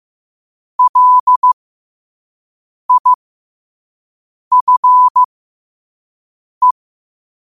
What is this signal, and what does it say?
Morse code, heard as a keyed tone, about 15 wpm, spaced as for 5 wpm, 1000 Hz. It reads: LIFE
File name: qotdfiles / 5wpm